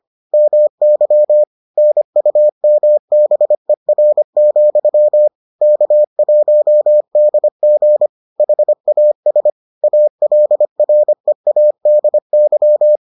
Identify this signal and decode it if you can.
Morse code, heard as a keyed tone, about 25 wpm, 600 Hz. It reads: MY NUMBER, K1DG HAS ALREADY